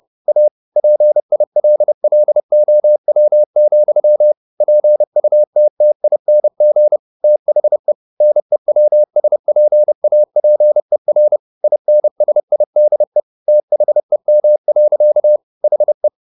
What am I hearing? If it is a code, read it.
Morse code, heard as a keyed tone, about 30 wpm, 600 Hz. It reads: A PILLOW, PUTTING THE NEWSPAPER INSIDE THEM. HE
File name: the old man and the sea / the old man and the sea 1 30wpm